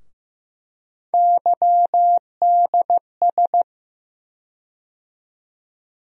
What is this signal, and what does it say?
Morse code, heard as a keyed tone, about 15 wpm, 700 Hz. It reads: YDS
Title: USA Call Sign Postfixes - 3 Letters 15wpm